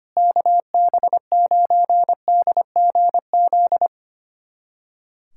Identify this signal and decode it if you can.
Morse code, heard as a keyed tone, about 25 wpm, 700 Hz. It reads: KB9DGZ
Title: USA Callsign DE Callsign - 6 Characters - 25wpm